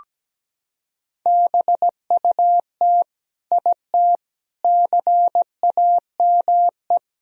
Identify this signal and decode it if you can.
Morse code, heard as a keyed tone, about 17 wpm, 700 Hz. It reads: BUT IT CAME
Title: Sentences from Top 1000 Words - Review Part 2 - 17wpm